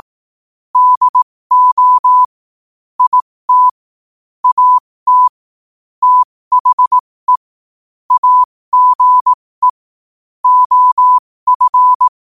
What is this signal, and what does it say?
Morse code, heard as a keyed tone, about 18 wpm, spaced as for 15 wpm, 1000 Hz. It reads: DO IT AT THE AGE OF
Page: New quotes every day in morse code at 15 Words per minute.